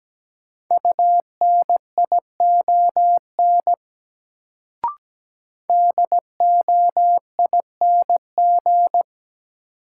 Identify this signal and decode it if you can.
Morse code, heard as a keyed tone, about 17 wpm, 700 Hz. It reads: UNION DOING
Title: Sentences from Top 1400 Words - New Word Drill - 17wpm